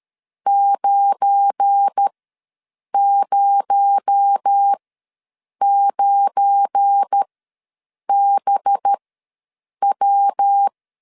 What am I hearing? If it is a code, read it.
Morse code, heard as a keyed tone, about 13 wpm, spaced as for 7 wpm, 800 Hz. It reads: 909BW
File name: LEZIONI / 7 GF9Z / 7 GF9 ws 12wpm 3344